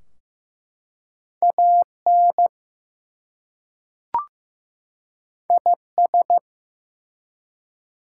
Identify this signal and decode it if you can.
Morse code, heard as a keyed tone, about 15 wpm, 700 Hz. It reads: AN IS
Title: Two Letter Words - 15wpm